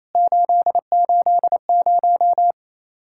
Morse code, heard as a keyed tone, about 28 wpm, 700 Hz. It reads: 880
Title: Worked All Europe DX Contest - QTC Exchange - 28wpm